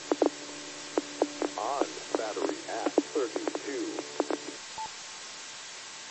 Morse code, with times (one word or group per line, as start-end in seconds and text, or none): none